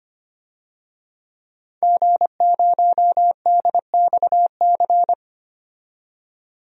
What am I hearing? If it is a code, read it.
Morse code, heard as a keyed tone, about 25 wpm, 700 Hz. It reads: G0DXC
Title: DXCC-100 Call Signs - 5 Characters - 25wpm